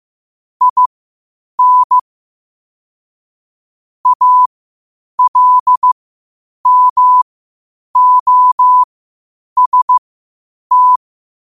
Morse code, heard as a keyed tone, about 15 wpm, spaced as for 8 wpm, 1000 Hz. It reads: IN ALMOST
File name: qotdfiles / 8wpm